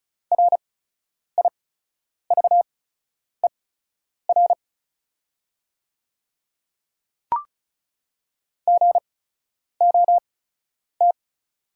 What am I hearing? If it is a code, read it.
Morse code, heard as a keyed tone, about 36 wpm, spaced as for 10 wpm, 700 Hz. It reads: RIVER GOT
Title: 8x Character Spacing - Top 300 Words - 35wpm